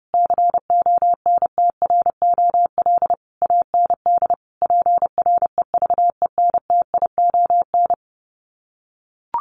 Morse code, heard as a keyed tone, about 30 wpm, 700 Hz. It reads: CONTROL AND PREVENTION
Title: Sets of 5 Words Encore2 30wpm